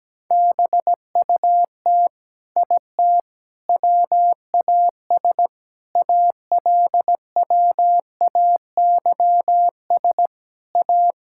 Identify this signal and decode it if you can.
Morse code, heard as a keyed tone, about 17 wpm, 700 Hz. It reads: BUT IT WAS ALWAYS A
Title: Sentences from Top 1600 Words 17wpm